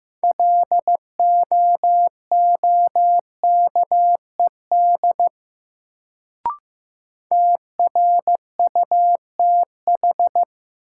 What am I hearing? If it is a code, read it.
Morse code, heard as a keyed tone, about 15 wpm, 700 Hz. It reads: LOOKED TRUTH
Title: Sentences from Top 800 Words - New Word Drill - 15wpm